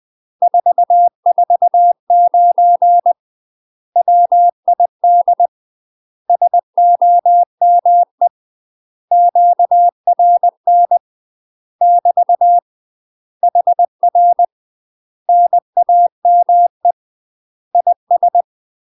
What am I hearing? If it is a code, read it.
Morse code, heard as a keyed tone, about 20 wpm, 700 Hz. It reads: 449 WID SOME QRN <BT> HR NAME IS
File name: UK MASTER QSOs TRAINING / 20WPM / UK MASTER QSOs 0047